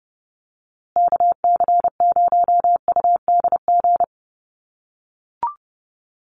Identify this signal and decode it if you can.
Morse code, heard as a keyed tone, about 30 wpm, 700 Hz. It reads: KC0UDG
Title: USA Call Signs Repeated 3x 30wpm